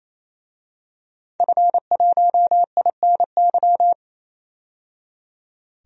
Morse code, heard as a keyed tone, about 28 wpm, 700 Hz. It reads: F1INY
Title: DXCC-100 Call Signs - 5 Characters - 28wpm